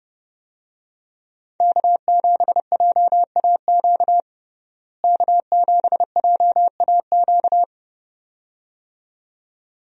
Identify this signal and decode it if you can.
Morse code, heard as a keyed tone, about 32 wpm, 700 Hz. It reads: K7JAQ K7JAQ
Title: USA Call Signs - 5 Characters - Repeated 2x - 30wpm